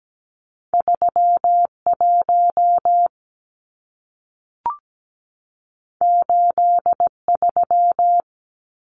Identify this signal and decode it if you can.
Morse code, heard as a keyed tone, about 17 wpm, 700 Hz. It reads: 31 83
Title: Numbers - 2-Digit 17wpm